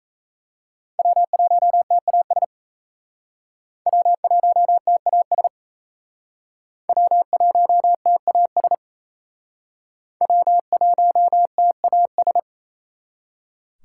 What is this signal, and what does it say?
Morse code, heard as a keyed tone, about 37 wpm, 700 Hz. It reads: W1TAS W1TAS W1TAS W1TAS